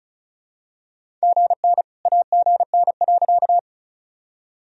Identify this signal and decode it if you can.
Morse code, heard as a keyed tone, about 36 wpm, 700 Hz. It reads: GN AGN.
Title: Sentences from Top 300 Words and Abbreviations 35wpm